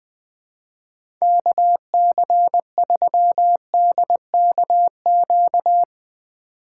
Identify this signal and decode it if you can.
Morse code, heard as a keyed tone, about 22 wpm, 700 Hz. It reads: KC3DKQ
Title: Speed Racing - USA 2x3 Call Signs 20wpm